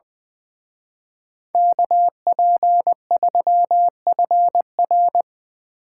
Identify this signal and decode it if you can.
Morse code, heard as a keyed tone, about 23 wpm, 700 Hz. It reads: KP3FR
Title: Speed Racing - USA 2x2 Call Signs 20wpm